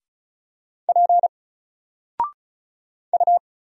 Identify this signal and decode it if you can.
Morse code, heard as a keyed tone, about 36 wpm, 700 Hz. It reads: P U
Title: Single Letter-Number 35wpm